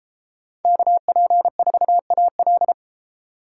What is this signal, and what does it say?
Morse code, heard as a keyed tone, about 31 wpm, 700 Hz. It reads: KP4AL